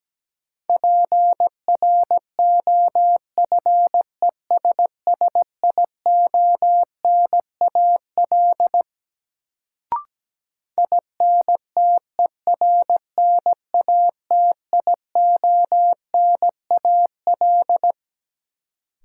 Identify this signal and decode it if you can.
Morse code, heard as a keyed tone, about 17 wpm, 700 Hz. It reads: PROFESSIONAL INTERNATIONAL
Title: Sentences from Top 1000 Words - Review Part 4 - New Word Drill - 17wpm